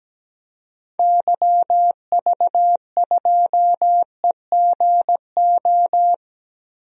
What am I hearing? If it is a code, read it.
Morse code, heard as a keyed tone, about 17 wpm, 700 Hz. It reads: YV2EGO
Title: DXCC-100 Call Signs - 6 Characters - 17wpm